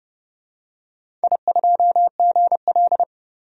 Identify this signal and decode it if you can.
Morse code, heard as a keyed tone, about 30 wpm, 700 Hz. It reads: I2GL
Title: DXCC-100 Call Signs - 4 Characters - 30wpm